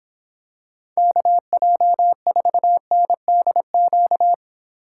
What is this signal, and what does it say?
Morse code, heard as a keyed tone, about 26 wpm, 700 Hz. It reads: KJ4NDQ